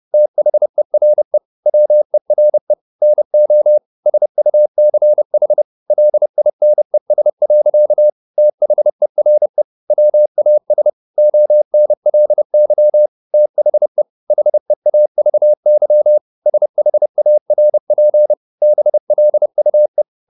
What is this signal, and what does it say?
Morse code, heard as a keyed tone, about 30 wpm, 600 Hz. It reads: THERE WERE NO SUCH LINES. THERE WAS ONLY THE HEAVY SHARP BLUE